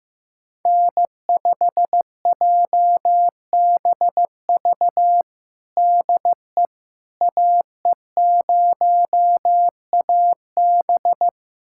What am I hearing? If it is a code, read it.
Morse code, heard as a keyed tone, about 15 wpm, 700 Hz. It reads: N5JBV DE AE0AB
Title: USA Callsign DE Callsign - 5 Characters - 15wpm